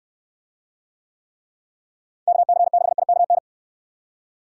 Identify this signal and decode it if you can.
Morse code, heard as a keyed tone, about 69 wpm, 700 Hz. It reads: KC6IKN